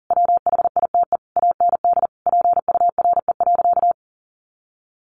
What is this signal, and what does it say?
Morse code, heard as a keyed tone, about 40 wpm, 700 Hz. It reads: WHITE AND PURE.
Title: Sentences from Top 1500 Words 40wpm